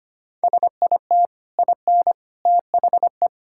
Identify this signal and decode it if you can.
Morse code, heard as a keyed tone, about 25 wpm, 700 Hz. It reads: SIT IN THE
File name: Easy Sentences with Hints - 25wpm